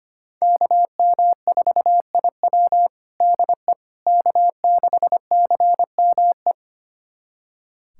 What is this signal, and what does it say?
Morse code, heard as a keyed tone, about 25 wpm, 700 Hz. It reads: KM4IW DE K6CME
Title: USA Callsign DE Callsign - 5 Characters - 25wpm